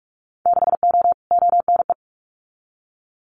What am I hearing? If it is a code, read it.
Morse code, heard as a keyed tone, about 46 wpm, 700 Hz. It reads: 6O ONE
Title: Winter Field Day Exchanges Encore 45wpm